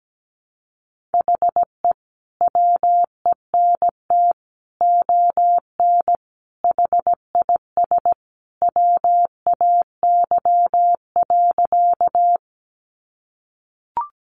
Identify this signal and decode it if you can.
Morse code, heard as a keyed tone, about 17 wpm, 700 Hz. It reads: HE WENT ON HIS WAY.
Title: Sentences from Top 200 Words 17wpm